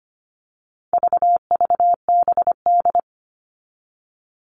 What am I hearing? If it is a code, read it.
Morse code, heard as a keyed tone, about 25 wpm, 700 Hz. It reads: VVBD
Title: All Cut Numbers - 4-Digit 25wpm